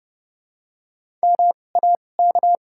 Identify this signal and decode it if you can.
Morse code, heard as a keyed tone, about 30 wpm, spaced as for 22 wpm, 700 Hz. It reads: MAK
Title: 2x Character Spacing - Top 200 Words - 30wpm